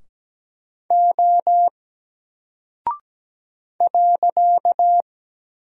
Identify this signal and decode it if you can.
Morse code, heard as a keyed tone, about 17 wpm, 700 Hz. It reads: O .
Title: Single Character (Letter:Number:Punctuation:Prosign) Alt 17wpm